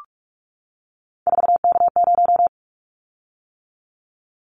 Morse code, heard as a keyed tone, about 44 wpm, 700 Hz. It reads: 4K0
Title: Callsign Prefixes for Top 100 DXCC - 45wpm